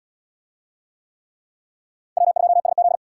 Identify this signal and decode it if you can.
Morse code, heard as a keyed tone, about 71 wpm, 700 Hz. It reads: W2SZ